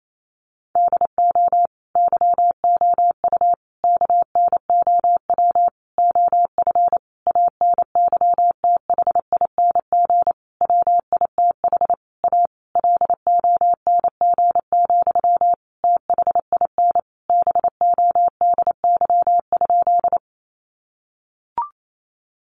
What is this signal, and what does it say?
Morse code, heard as a keyed tone, about 28 wpm, 700 Hz. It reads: DO YOU KNOW OF ANYTHING WITH A LONG, THIN BODY?
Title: Sentences from Top 1200 Words 28wpm